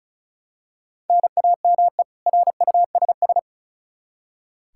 Morse code, heard as a keyed tone, about 34 wpm, 700 Hz. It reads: NAME RUSS
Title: Practice QSO - Exchange 1 - Beginning - 35wpm